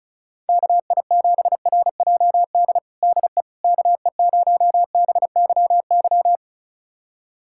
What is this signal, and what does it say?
Morse code, heard as a keyed tone, about 34 wpm, 700 Hz. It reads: KI7RJD DE KE0BYY
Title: USA Callsign DE Callsign - 6 Characters - 35wpm